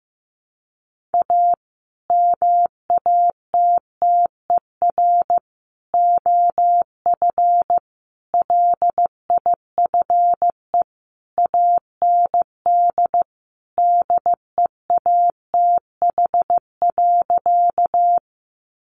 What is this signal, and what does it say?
Morse code, heard as a keyed tone, about 15 wpm, 700 Hz. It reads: A MATTER OF LIFE AND DEATH.
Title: Sentences from Top 1100 Words 15wpm